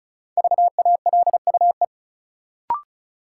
Morse code, heard as a keyed tone, about 36 wpm, 700 Hz. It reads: VALUE
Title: Sentences from Top 600 Words - New Word Drill - 35wpm